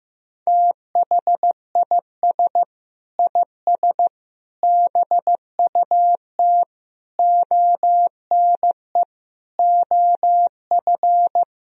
Morse code, heard as a keyed tone, about 15 wpm, 700 Hz. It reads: THIS IS BUT ONE OF
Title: Sentences from Top 1000 Words - Review Part 2 - 15wpm